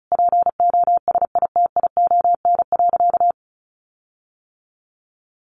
Morse code, heard as a keyed tone, about 34 wpm, 700 Hz. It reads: POSITION.
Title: Sentences from Top 1400 Words 35wpm